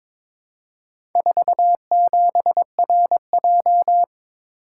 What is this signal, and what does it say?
Morse code, heard as a keyed tone, about 22 wpm, 700 Hz. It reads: 47RJ
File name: practice / 22wpm / Four Letter-Number Combinations - 22wpm